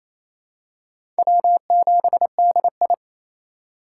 Morse code, heard as a keyed tone, about 28 wpm, 700 Hz. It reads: W7DI
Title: USA Call Signs - 2x Character Spacing - 28wpm